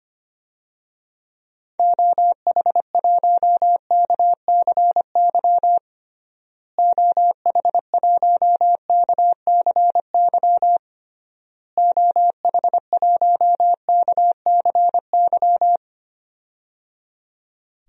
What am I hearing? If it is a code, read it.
Morse code, heard as a keyed tone, about 25 wpm, 700 Hz. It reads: OH1KCY OH1KCY OH1KCY
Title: DXCC-100 Call Signs - 6 Characters - Repeated 3x - 25wpm